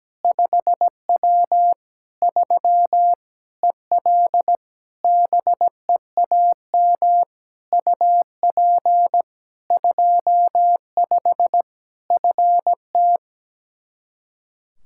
Morse code, heard as a keyed tone, about 17 wpm, 700 Hz. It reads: 5W 3 EL BEAM UP 25 FT
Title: Practice QSO - Exchange 2 - Info - 17wpm